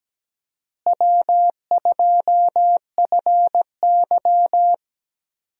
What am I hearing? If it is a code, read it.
Morse code, heard as a keyed tone, about 17 wpm, 700 Hz. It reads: W2FY
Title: USA Call Signs - 4 Characters - Repeated 3x - 17wpm